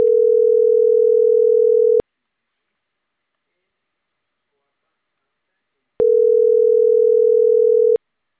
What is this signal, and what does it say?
Ringback tone (North America), 440 + 480 Hz.